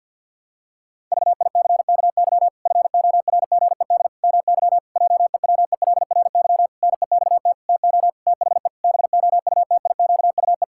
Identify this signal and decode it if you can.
Morse code, heard as a keyed tone, about 50 wpm, 700 Hz. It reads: VICKY FORGED MY JEWELRY NEXT TO THE BOUTIQUE
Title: Pangram Challenge v2 - 50wpm